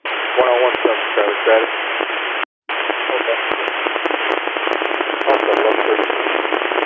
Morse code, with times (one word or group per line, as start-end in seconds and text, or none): none